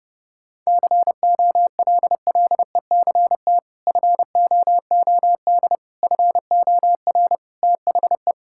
Morse code, heard as a keyed tone, about 30 wpm, 700 Hz. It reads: COLLECT FOOD FOR THE
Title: Sentences from Top 1600 Words 30wpm